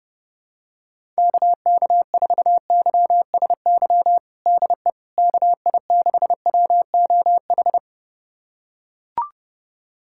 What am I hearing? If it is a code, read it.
Morse code, heard as a keyed tone, about 30 wpm, 700 Hz. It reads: KK4YSY DE KI6WOH
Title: USA Callsign DE Callsign 30wpm